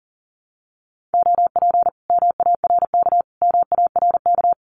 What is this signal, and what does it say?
Morse code, heard as a keyed tone, about 40 wpm, 700 Hz. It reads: OP MARK MARK
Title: Common QSO Phrases Encore 40wpm